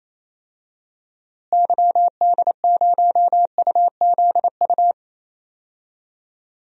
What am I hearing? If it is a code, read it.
Morse code, heard as a keyed tone, about 27 wpm, 700 Hz. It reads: YD0UZU
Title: DXCC-100 Call Signs - 6 Characters - Repeated 3x - 28wpm